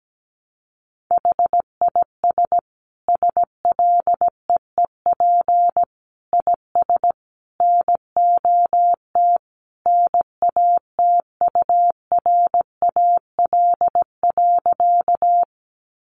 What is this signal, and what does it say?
Morse code, heard as a keyed tone, about 17 wpm, 700 Hz. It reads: HIS SLEEP IS NOT NATURAL.
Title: Sentences from Top 700 Words 17wpm